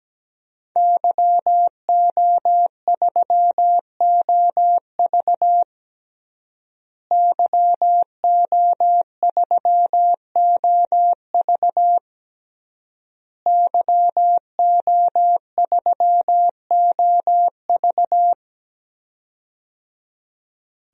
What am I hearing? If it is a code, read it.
Morse code, heard as a keyed tone, about 17 wpm, 700 Hz. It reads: YO3OV YO3OV YO3OV
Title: DXCC-100 Call Signs - 5 Characters - Repeated 3x - 17wpm